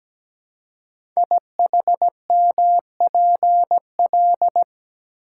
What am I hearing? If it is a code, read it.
Morse code, heard as a keyed tone, about 17 wpm, 700 Hz. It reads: IHMPL
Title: Code Groups - 5 Letters - 17wpm